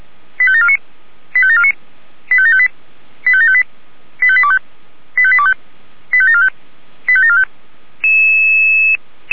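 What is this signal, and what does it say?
Five-tone selective call (ZVEI), heard as quick runs of five tones, tones of about 70 ms. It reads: 85749 85749 85758 85758 85825 85825 85745 85745
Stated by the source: ZVEI five-tone selective call received by SDR.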